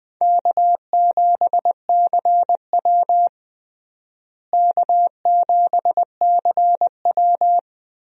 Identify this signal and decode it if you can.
Morse code, heard as a keyed tone, about 20 wpm, 700 Hz. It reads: K7CW K7CW
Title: USA Call Signs - 4 Characters - Repeated 2x - 20wpm